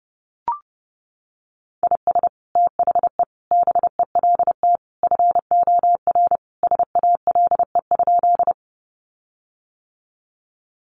Morse code, heard as a keyed tone, about 30 wpm, 700 Hz. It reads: IS THE BELT FOR SALE?
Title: Sentences from Top 1400 Words 30wpm